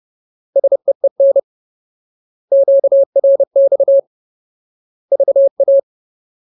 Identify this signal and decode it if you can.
Morse code, heard as a keyed tone, about 30 wpm, 550 Hz. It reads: SEEN QRX VA
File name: Ultimate Word List / 30WPM Ultimate